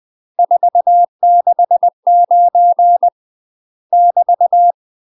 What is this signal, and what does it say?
Morse code, heard as a keyed tone, about 20 wpm, 700 Hz. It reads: 469 <BT>
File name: UK MASTER QSOs TRAINING / 20WPM / UK MASTER QSOs 0014